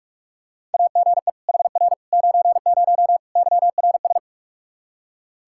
Morse code, heard as a keyed tone, about 43 wpm, 700 Hz. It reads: AGE HR 90 YRS
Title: Practice QSO - Exchange 3 - More Info - 45wpm